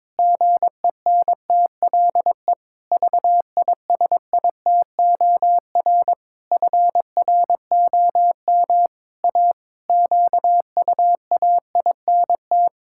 Morse code, heard as a keyed tone, about 22 wpm, 700 Hz. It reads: GENTLE VISITOR FROM A QUAINT
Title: Pangram Challenge v2 - 22wpm